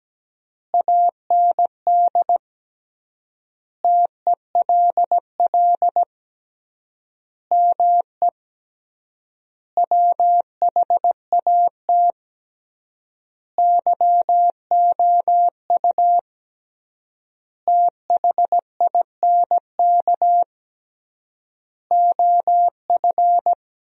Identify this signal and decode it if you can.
Morse code, heard as a keyed tone, about 17 wpm, 700 Hz. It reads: AND TELL ME WHAT YOU THINK OF
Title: 3x Word-Spacing - Sentences from Top 200 Words - 17wpm